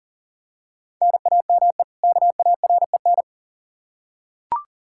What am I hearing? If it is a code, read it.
Morse code, heard as a keyed tone, about 38 wpm, 700 Hz. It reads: NAME KAREN
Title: Practice QSO - Exchange 1 - Beginning - 40wpm